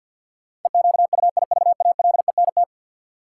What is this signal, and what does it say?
Morse code, heard as a keyed tone, about 53 wpm, 700 Hz. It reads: EQUIVALENT
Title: Sentences from Top 1600 Words - New Word Drill - 50wpm